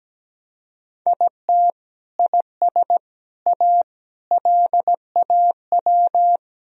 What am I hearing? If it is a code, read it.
Morse code, heard as a keyed tone, about 17 wpm, 700 Hz. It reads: IT IS A LAW
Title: Sentences from Top 1000 Words - Review Part 3 - 17wpm